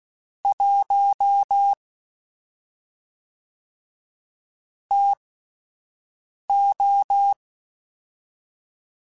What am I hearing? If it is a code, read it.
Morse code, heard as a keyed tone, about 16 wpm, spaced as for 5 wpm, 800 Hz. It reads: 1 TO
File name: Spaced Characters / Spaced Characters-mp3s / Weather5 5 sp @10